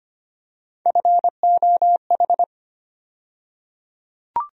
Morse code, heard as a keyed tone, about 25 wpm, 700 Hz. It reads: FOH
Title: USA Call Sign Postfixes - 3 Letters 25wpm